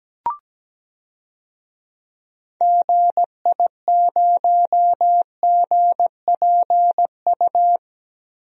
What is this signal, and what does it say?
Morse code, heard as a keyed tone, about 17 wpm, 700 Hz. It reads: GI0GPU
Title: DXCC-100 Call Signs - 6 Characters - 17wpm